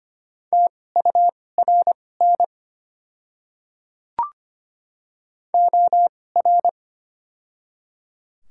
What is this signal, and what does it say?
Morse code, heard as a keyed tone, about 25 wpm, spaced as for 18 wpm, 690 Hz. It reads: TURN OR
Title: 2x Character Spacing - Top 200 Words - 25wpm